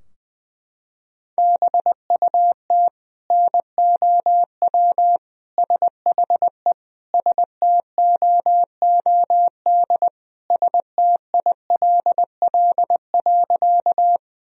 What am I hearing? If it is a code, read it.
Morse code, heard as a keyed tone, about 20 wpm, 700 Hz. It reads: BUT NOW SHE STOOD STILL.
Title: Sentences from Top 1000 Words - Review Part 2 - 20wpm